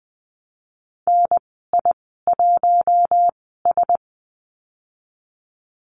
Morse code, heard as a keyed tone, about 20 wpm, spaced as for 14 wpm, 700 Hz. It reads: NI1S
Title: 048 Lesson 9g - Number 4 - Call Signs 20wpm